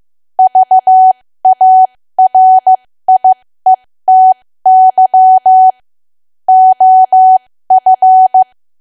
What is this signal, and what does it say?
Morse code, heard as a keyed tone, about 15 wpm, spaced as for 13 wpm, 750 Hz. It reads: VARIETY OF
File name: cwa / W1AW-Level2 / 13WPM